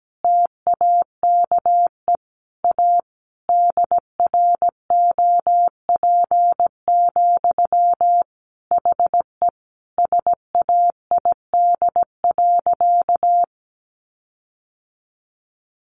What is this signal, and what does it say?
Morse code, heard as a keyed tone, about 17 wpm, 700 Hz. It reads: TAKE A DROP, HE SAID.
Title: Sentences from Top 1400 Words 17wpm